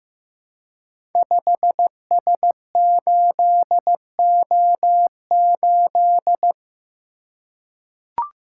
Morse code, heard as a keyed tone, about 15 wpm, 700 Hz. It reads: 5S8O8
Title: Code Groups - 5 Letter:Numbers - 15wpm